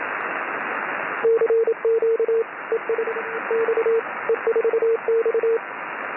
Morse code, heard as a keyed tone, about 27 wpm, 450 Hz, with radio noise. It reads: CQ ES X E4X
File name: E4X CW